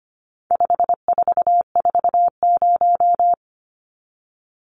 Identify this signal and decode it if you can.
Morse code, heard as a keyed tone, about 25 wpm, 700 Hz. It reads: 5440
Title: Numbers - 4-Digit 25wpm